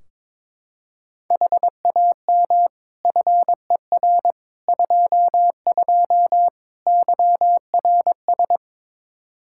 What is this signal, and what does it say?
Morse code, heard as a keyed tone, about 22 wpm, 700 Hz. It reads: HAM FER 22 YRS
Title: Practice QSO - Exchange 3 - More Info - 22wpm